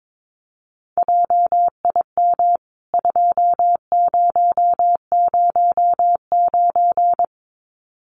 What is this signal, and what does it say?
Morse code, heard as a keyed tone, about 22 wpm, 700 Hz. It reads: JIM 2009
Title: CWT Exchange 22wpm